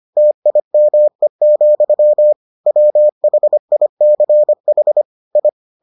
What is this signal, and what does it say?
Morse code, heard as a keyed tone, about 25 wpm, 600 Hz. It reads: TIME, WHICH I